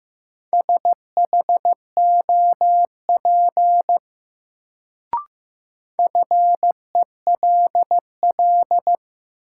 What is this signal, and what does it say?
Morse code, heard as a keyed tone, about 15 wpm, 700 Hz. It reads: SHOP FELL
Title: Sentences from Top 1600 Words - New Word Drill - 15wpm